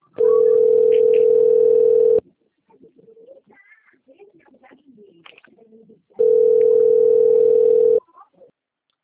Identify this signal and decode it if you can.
Ringback tone (North America), 440 + 480 Hz.
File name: rg-1600-unknown- - - .12227